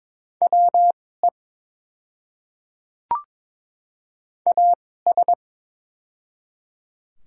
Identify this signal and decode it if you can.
Morse code, heard as a keyed tone, about 22 wpm, spaced as for 16 wpm, 700 Hz. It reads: WE AS
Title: 2x Character Spacing - Top 100 Words - 22wpm